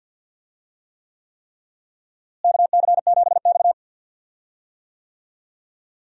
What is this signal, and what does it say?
Morse code, heard as a keyed tone, about 51 wpm, 700 Hz. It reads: KC7X
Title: USA Call Signs - 4 Characters - Repeated 3x - 50wpm